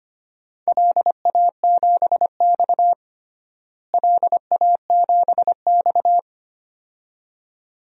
Morse code, heard as a keyed tone, about 25 wpm, 700 Hz. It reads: LA7X LA7X
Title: DXCC-100 Call Signs - 4 Characters - Repeated 2x - 25wpm